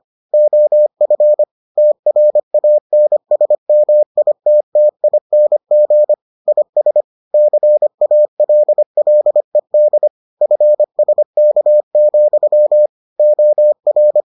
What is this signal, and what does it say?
Morse code, heard as a keyed tone, about 25 wpm, 600 Hz. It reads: OF TRANSMITTING IS CALLED FSK, OR